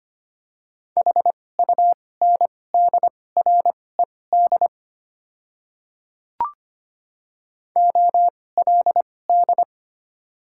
Morse code, heard as a keyed tone, about 25 wpm, spaced as for 18 wpm, 700 Hz. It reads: HUNDRED OLD
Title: 2x Character Spacing - Top 500 Words - 25wpm